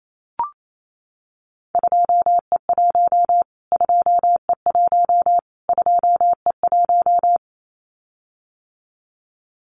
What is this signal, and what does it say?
Morse code, heard as a keyed tone, about 28 wpm, 700 Hz. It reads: 2E1 2E1 2E1
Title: Callsign Prefixes for Top 25 DXCC - Repeated 3x - 28wpm